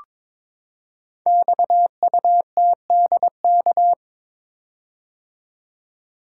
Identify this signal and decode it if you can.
Morse code, heard as a keyed tone, about 22 wpm, 700 Hz. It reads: XUTDK
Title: Code Groups - 5 Letters - 22wpm